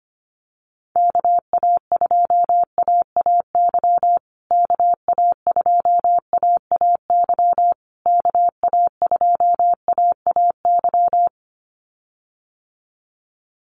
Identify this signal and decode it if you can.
Morse code, heard as a keyed tone, about 25 wpm, 700 Hz. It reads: KA2AAY KA2AAY KA2AAY
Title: USA Call Signs Repeated 3x 25wpm